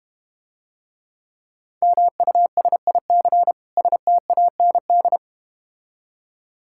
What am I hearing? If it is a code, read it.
Morse code, heard as a keyed tone, about 32 wpm, 700 Hz. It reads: MUSIC STAND